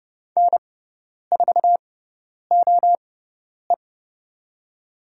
Morse code, heard as a keyed tone, about 31 wpm, spaced as for 10 wpm, 700 Hz. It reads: N4OE
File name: Lesson 18g - Letter m - Call Signs 30wpm